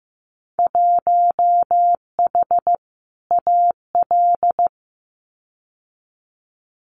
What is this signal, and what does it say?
Morse code, heard as a keyed tone, about 15 wpm, 700 Hz. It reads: 1H AL
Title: Winter Field Day Exchanges 15wpm